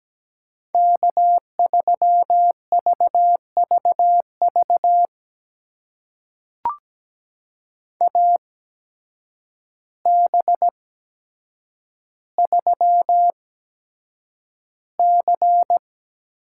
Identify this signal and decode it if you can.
Morse code, heard as a keyed tone, about 17 wpm, spaced as for 8 wpm, 700 Hz. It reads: K3VVV AB3C
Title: USA Call Signs - 5 Characters - 8x Character Spacing - 17wpm